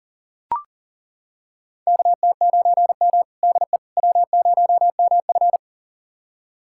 Morse code, heard as a keyed tone, about 40 wpm, 700 Hz. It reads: KT9M DE W0MF
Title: USA Callsign DE Callsign - 4 Characters - 40wpm